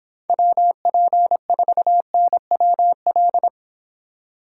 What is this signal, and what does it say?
Morse code, heard as a keyed tone, about 29 wpm, 700 Hz. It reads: WP4NWL